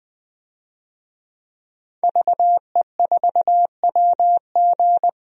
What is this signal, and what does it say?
Morse code, heard as a keyed tone, about 20 wpm, 700 Hz. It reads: VE4WG
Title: DXCC-100 Call Signs - 5 Characters - 20wpm